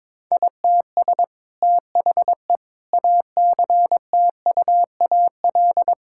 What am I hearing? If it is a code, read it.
Morse code, heard as a keyed tone, about 22 wpm, 700 Hz. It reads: ITS THE ACTUAL
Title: Sentences from Top 1000 Words - Review Part 5 - 22wpm